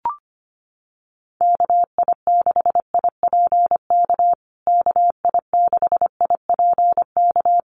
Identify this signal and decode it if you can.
Morse code, heard as a keyed tone, about 25 wpm, 700 Hz. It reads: KI6IPK KI6IPK
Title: USA Call Signs Repeated 2x 25wpm